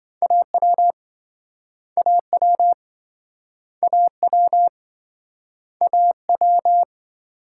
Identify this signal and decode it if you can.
Morse code, heard as a keyed tone, about 27 wpm, 700 Hz. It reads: AW AW AW AW